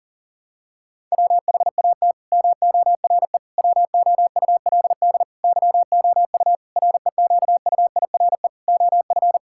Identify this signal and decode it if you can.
Morse code, heard as a keyed tone, about 40 wpm, 700 Hz. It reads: WHAT MORE WOULD YOU REQUIRE OF
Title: Sentences from Top 1000 Words - Review Part 6 - 40wpm